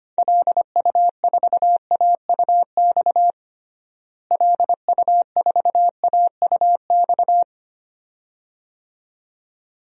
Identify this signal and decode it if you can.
Morse code, heard as a keyed tone, about 25 wpm, 700 Hz. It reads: LU4AUX LU4AUX
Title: DXCC-100 Call Signs - 6 Characters - Repeated 2x - 25wpm